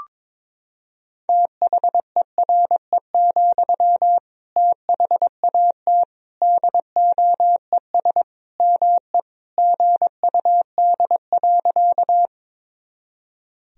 Morse code, heard as a keyed tone, about 22 wpm, 700 Hz. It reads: THERE, THAT DOES ME GUD.
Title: Sentences from Top 200 Words and Abbreviations 22wpm